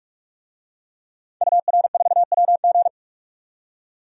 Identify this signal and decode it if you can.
Morse code, heard as a keyed tone, about 48 wpm, 700 Hz. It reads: UR4WG
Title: DXCC-100 Call Signs - 5 Characters - 45wpm